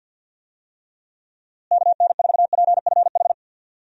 Morse code, heard as a keyed tone, about 51 wpm, 700 Hz. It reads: KN4PFH
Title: USA Call Signs - 6 Characters - Repeated 3x - 50wpm